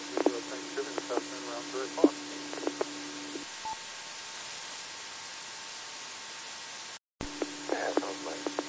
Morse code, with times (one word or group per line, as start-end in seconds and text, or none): none